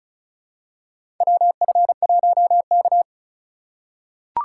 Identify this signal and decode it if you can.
Morse code, heard as a keyed tone, about 36 wpm, 700 Hz. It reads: WF1K
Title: USA Call Signs - 8x Character Spacing - 35wpm